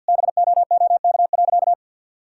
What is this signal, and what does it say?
Morse code, heard as a keyed tone, about 50 wpm, 700 Hz. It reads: BOOK.
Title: Sentences from Top 1000 Words - Review Part 6 - 50wpm